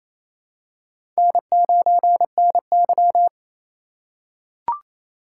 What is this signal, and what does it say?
Morse code, heard as a keyed tone, about 29 wpm, 700 Hz. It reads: N9NY
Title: USA Call Signs - 6x Character Spacing - 28wpm